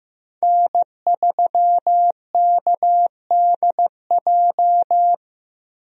Morse code, heard as a keyed tone, about 15 wpm, 700 Hz. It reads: N3KDJ
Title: USA Call Signs - 5 Characters - Repeated 2x - 15wpm